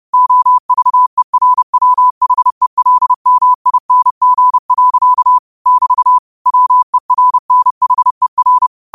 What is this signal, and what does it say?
Morse code, heard as a keyed tone, about 30 wpm, 1000 Hz. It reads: OVERWHELMING. <BT> WERNHER
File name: qotdfiles / 30wpm